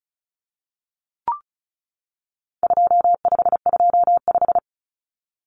Morse code, heard as a keyed tone, about 34 wpm, 700 Hz. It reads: 2525
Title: Common Cut Numbers - 4-Digit 35wpm